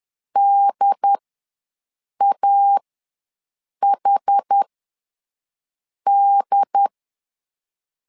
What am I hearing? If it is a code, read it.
Morse code, heard as a keyed tone, about 10 wpm, spaced as for 5 wpm, 800 Hz. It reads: DAHD
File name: LEZIONI / 3 MWRO / 3 MW ws 10wpm 2213